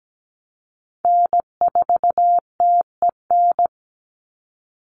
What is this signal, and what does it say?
Morse code, heard as a keyed tone, about 17 wpm, 700 Hz. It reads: N4TEN
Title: USA Call Signs 17wpm